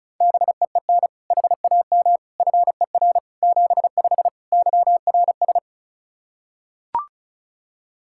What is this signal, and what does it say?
Morse code, heard as a keyed tone, about 34 wpm, 700 Hz. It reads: BEEN HAM FER 75 YRS
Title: Practice QSO - Exchange 3 - More Info - 35wpm